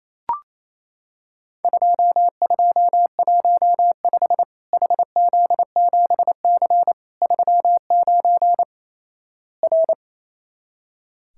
Morse code, heard as a keyed tone, about 28 wpm, 700 Hz. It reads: 2215 HZ7C 39 R
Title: Worked All Europe DX Contest - QTC Exchange - 28wpm